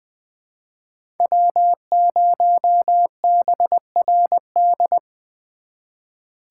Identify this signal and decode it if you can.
Morse code, heard as a keyed tone, about 22 wpm, 700 Hz. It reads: W0BRD
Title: Speed Racing - USA 1x3 Call Signs 20wpm